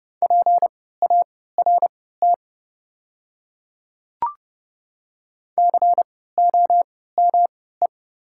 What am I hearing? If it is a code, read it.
Morse code, heard as a keyed tone, about 30 wpm, spaced as for 17 wpm, 700 Hz. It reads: PART COME
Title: 3x Character Spacing - Top 200 Words - 30wpm